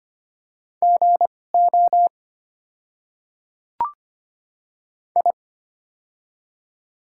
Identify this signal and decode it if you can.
Morse code, heard as a keyed tone, about 25 wpm, spaced as for 18 wpm, 700 Hz. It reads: GO I
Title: 2x Character Spacing - Top 100 Words - 25wpm